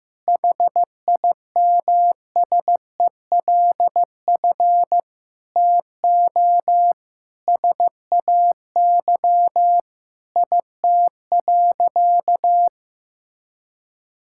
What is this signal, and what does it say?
Morse code, heard as a keyed tone, about 16 wpm, 700 Hz. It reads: HIMSELF TO SAY IT.
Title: Sentences from Top 1600 Words 15wpm